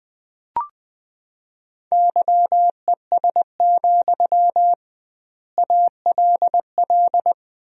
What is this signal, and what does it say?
Morse code, heard as a keyed tone, about 20 wpm, 700 Hz. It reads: YES, ALL
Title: 2x Word-Spacing - Sentences from Top 500 Words - 20wpm